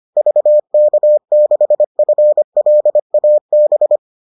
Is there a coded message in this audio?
Yes: Morse code from 0.2 to 4.0 s, heard as a keyed tone, about 25 wpm, 600 Hz. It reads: VK6FLAB